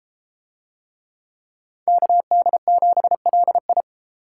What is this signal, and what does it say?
Morse code, heard as a keyed tone, about 32 wpm, 700 Hz. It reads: KD7LI